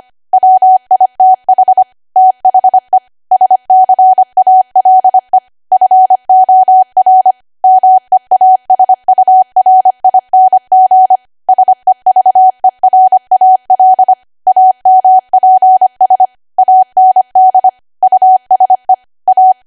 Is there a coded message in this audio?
Yes: Morse code from 0.3 to 19.5 s, heard as a keyed tone, about 25 wpm, 750 Hz. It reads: WITH THE SCALE FOR MEASURING SEVERAL AMPS AND USE A